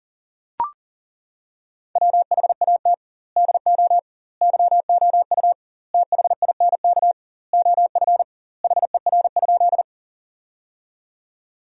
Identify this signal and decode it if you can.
Morse code, heard as a keyed tone, about 40 wpm, 700 Hz. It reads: WHAT DO YOU THINK OF HER?
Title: 2x Word-Spacing - Sentences from Top 200 Words - 40wpm